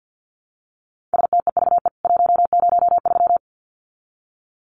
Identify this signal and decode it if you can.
Morse code, heard as a keyed tone, about 50 wpm, 700 Hz. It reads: STEVE 103
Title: CWT Exchange 50wpm